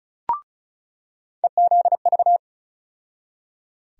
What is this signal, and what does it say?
Morse code, heard as a keyed tone, about 36 wpm, 700 Hz. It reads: EZV
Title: USA Call Sign Postfixes - 3 Letters 35wpm